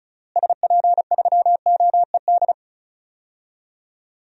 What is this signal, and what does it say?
Morse code, heard as a keyed tone, about 34 wpm, 700 Hz. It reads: SP3OED
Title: DXCC-100 Call Signs - 6 Characters - Repeated 3x - 35wpm